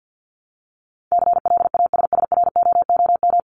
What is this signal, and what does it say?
Morse code, heard as a keyed tone, about 50 wpm, 700 Hz. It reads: CLASSROOM
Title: Top 1251-1500 Words 50wpm